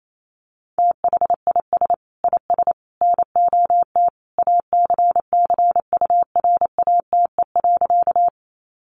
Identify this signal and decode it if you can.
Morse code, heard as a keyed tone, about 28 wpm, 700 Hz. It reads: THIS IS NOT ACCURATE.
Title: Sentences from Top 1200 Words 28wpm